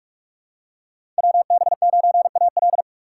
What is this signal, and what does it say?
Morse code, heard as a keyed tone, about 42 wpm, 700 Hz. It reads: WB9AL
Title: USA Call Signs - 3x Character Spacing - 45wpm